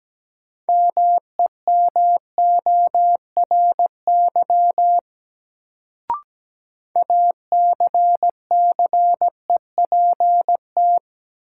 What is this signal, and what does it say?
Morse code, heard as a keyed tone, about 17 wpm, 700 Hz. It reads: MEMORY ACCEPT
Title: Sentences from Top 1300 Words - New Word Drill - 17wpm